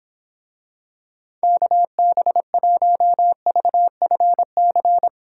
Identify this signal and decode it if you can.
Morse code, heard as a keyed tone, about 26 wpm, 700 Hz. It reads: KB1VFC